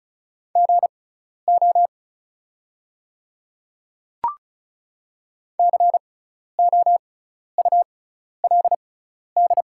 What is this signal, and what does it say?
Morse code, heard as a keyed tone, about 36 wpm, spaced as for 12 wpm, 700 Hz. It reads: GO COULD
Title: 6x Character Spacing - Top 100 Words - 35wpm